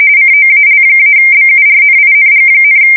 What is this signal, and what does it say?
RTTY, about 45.45 baud. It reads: COAL-BURNING,
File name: fsk170 / battleshipnj